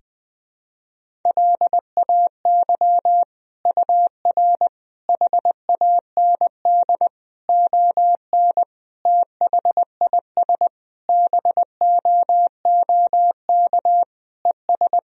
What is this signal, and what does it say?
Morse code, heard as a keyed tone, about 20 wpm, 700 Hz. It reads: LAY UR HAND ON THIS BOOK ES
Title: Sentences from Top 400 Words and Abbreviations 20wpm